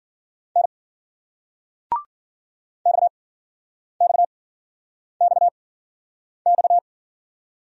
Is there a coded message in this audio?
Yes: Morse code from 0.6 to 6.8 s, heard as a keyed tone, about 56 wpm, spaced as for 9 wpm, 700 Hz. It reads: N XXXX